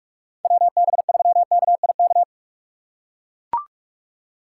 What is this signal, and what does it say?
Morse code, heard as a keyed tone, about 43 wpm, 700 Hz. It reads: WB3KIK
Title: USA Call Signs - 6 Characters - 6x Character Spacing - 45wpm